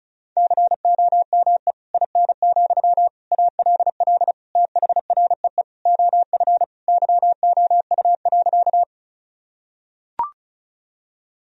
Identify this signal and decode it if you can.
Morse code, heard as a keyed tone, about 36 wpm, 700 Hz. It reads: COME IN, ALL THREE OF YOU.
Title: Sentences from Top 1000 Words - Review Part 1 - 35wpm